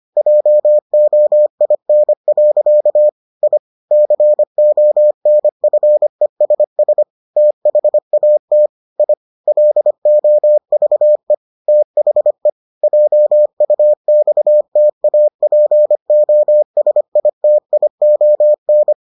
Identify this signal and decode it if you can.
Morse code, heard as a keyed tone, about 25 wpm, 600 Hz. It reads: JOIN. I CONFESS THAT I LOVE THE JUXTAPOSITION